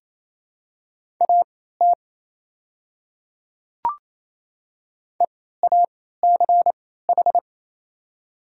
Morse code, heard as a keyed tone, about 27 wpm, spaced as for 16 wpm, 690 Hz. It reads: AT EACH
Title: 3x Character Spacing - Top 100 Words - 28wpm